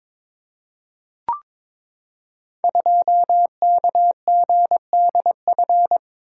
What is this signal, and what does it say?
Morse code, heard as a keyed tone, about 22 wpm, 700 Hz. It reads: 2KGDF
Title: Code Groups - 5 Letter:Numbers - 22wpm